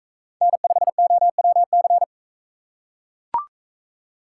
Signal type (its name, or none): Morse code